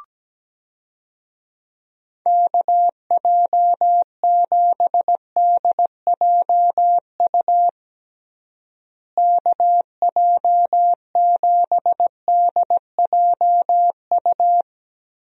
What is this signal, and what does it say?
Morse code, heard as a keyed tone, about 17 wpm, 700 Hz. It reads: KJ7DJU KJ7DJU
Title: USA Call Signs - 6 Characters - Repeated 3x - 17wpm